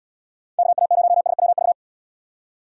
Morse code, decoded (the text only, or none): KA0SWV